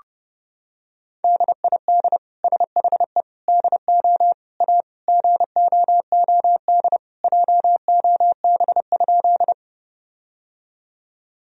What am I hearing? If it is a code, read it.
Morse code, heard as a keyed tone, about 32 wpm, 700 Hz. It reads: DID SHE DO A GOOD JOB?
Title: Sentences from Top 1000 Words - Review Part 3 - 30wpm